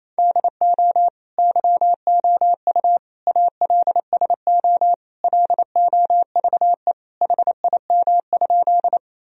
Morse code, heard as a keyed tone, about 27 wpm, 700 Hz. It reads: DO YOU ALSO LOVE HIM?
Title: Sentences from Top 1000 Words - Review Part 1 - 28wpm